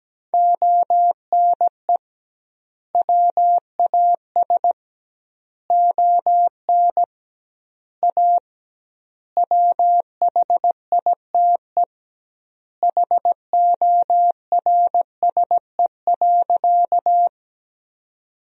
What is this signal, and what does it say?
Morse code, heard as a keyed tone, about 17 wpm, 700 Hz. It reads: ONE WAS ON A WHITE HORSE.
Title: 2x Word-Spacing - Sentences from Top 300 Words - 17wpm